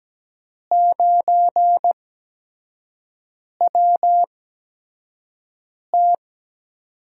Morse code, heard as a keyed tone, about 17 wpm, spaced as for 5 wpm, 700 Hz. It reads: 9WT
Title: USA Call Signs - 4 Characters - 8x Character Spacing - 17wpm